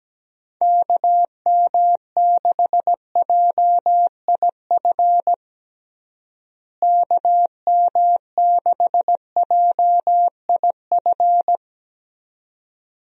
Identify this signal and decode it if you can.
Morse code, heard as a keyed tone, about 17 wpm, 700 Hz. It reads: KM6JIF KM6JIF
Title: USA Call Signs - 6 Characters - Repeated 2x - 17wpm